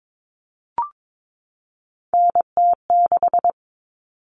Morse code, heard as a keyed tone, about 22 wpm, 700 Hz. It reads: NT6
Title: All Cut Numbers - 3-Digit 22wpm